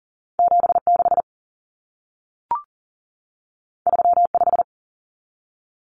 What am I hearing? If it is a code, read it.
Morse code, heard as a keyed tone, about 40 wpm, 700 Hz. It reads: 76 35
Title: Common Cut Numbers - 2-Digit 40wpm